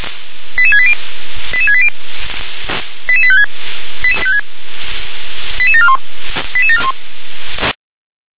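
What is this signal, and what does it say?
Five-tone selective call (ZVEI), heard as quick runs of five tones, tones of about 70 ms. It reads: 88680 88680 88856 88741 88741
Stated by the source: ZVEI five-tone selective call received with an SDR.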